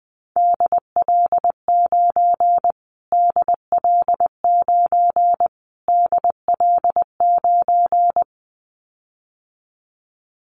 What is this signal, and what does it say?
Morse code, heard as a keyed tone, about 20 wpm, 700 Hz. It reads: DL9 DL9 DL9
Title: Callsign Prefixes for Top 25 DXCC - Repeated 3x - 20wpm